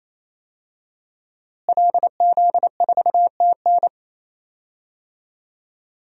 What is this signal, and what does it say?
Morse code, heard as a keyed tone, about 28 wpm, 700 Hz. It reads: LZ4TN
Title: DXCC-100 Call Signs - 5 Characters - Repeated 3x - 28wpm